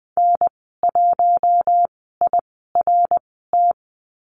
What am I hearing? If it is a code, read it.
Morse code, heard as a keyed tone, about 20 wpm, spaced as for 15 wpm, 700 Hz. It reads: N1IRT
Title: 055 Lesson 10g - Letter r - Call Signs 20wpm